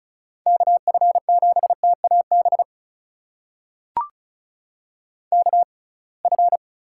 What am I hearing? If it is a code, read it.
Morse code, heard as a keyed tone, about 36 wpm, spaced as for 27 wpm, 700 Hz. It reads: KF7TAB KF
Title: USA Call Signs - 6 Characters - 6x Character Spacing - 35wpm